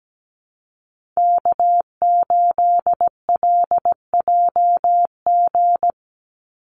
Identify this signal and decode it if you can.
Morse code, heard as a keyed tone, about 17 wpm, 700 Hz. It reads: K8LJG
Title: NA SOTA Chaser Callsigns 17wpm